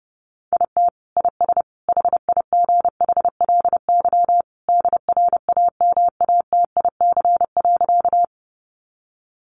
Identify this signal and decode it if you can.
Morse code, heard as a keyed tone, about 30 wpm, 700 Hz. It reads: IT IS HIGHLY DRAMATIC.
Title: Sentences from Top 1300 Words 30wpm